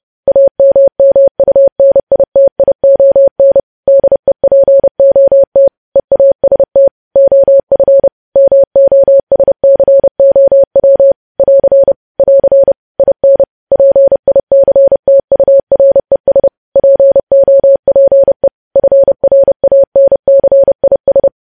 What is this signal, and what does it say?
Morse code, heard as a keyed tone, about 30 wpm, 550 Hz. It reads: AMMUNITION DEPOT EAST OF MOSCOW <AR> <AR> IN PICTURES POPE FRANCIS
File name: podcast / News headlines-1x-30x30